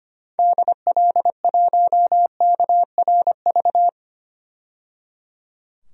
Morse code, heard as a keyed tone, about 25 wpm, 700 Hz. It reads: DL1KRV
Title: DXCC-100 Call Signs - 6 Characters - 25wpm